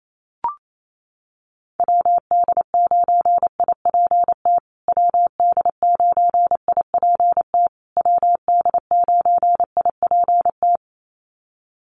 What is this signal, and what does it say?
Morse code, heard as a keyed tone, about 28 wpm, 700 Hz. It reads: WD9IPT WD9IPT WD9IPT
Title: USA Call Signs Repeated 3x 28wpm